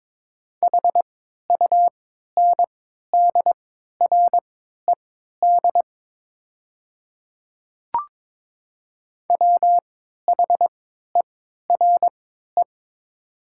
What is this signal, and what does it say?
Morse code, heard as a keyed tone, about 22 wpm, spaced as for 13 wpm, 700 Hz. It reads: HUNDRED WHERE
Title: 3x Character Spacing - Top 500 Words - 22wpm